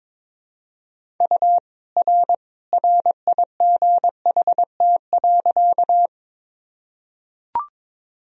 Morse code, heard as a keyed tone, about 21 wpm, 700 Hz. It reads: U R RIGHT.
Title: Sentences from Top 200 Words and Abbreviations 22wpm